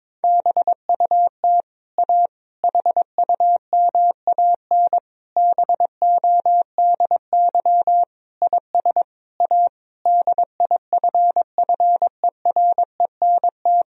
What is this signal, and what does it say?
Morse code, heard as a keyed tone, about 22 wpm, 690 Hz. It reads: BUT A HUMAN BODY IS A DIFFERENT
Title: Sentences from Top 1000 Words - Review Part 3 - 22wpm